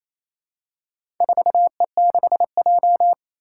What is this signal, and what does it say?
Morse code, heard as a keyed tone, about 28 wpm, 700 Hz. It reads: 4E6J
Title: Four Challenging Letter-Number Combinations 28wpm